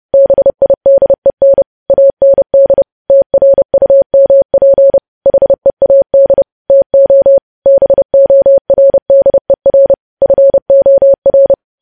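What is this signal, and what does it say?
Morse code, heard as a keyed tone, about 30 wpm, 550 Hz. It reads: BIDEN AND TRUMP HEAD TO BORDER FOR
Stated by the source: Morse news headlines sent at 30 wpm.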